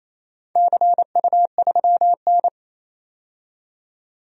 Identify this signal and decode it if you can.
Morse code, heard as a keyed tone, about 28 wpm, 700 Hz. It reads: CU3N
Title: DXCC-100 Call Signs - 4 Characters - 28wpm